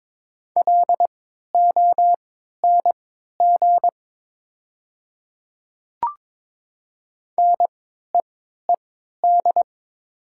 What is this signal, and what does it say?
Morse code, heard as a keyed tone, about 22 wpm, spaced as for 13 wpm, 700 Hz. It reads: LONG NEED
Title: 3x Character Spacing - Top 300 Words - 22wpm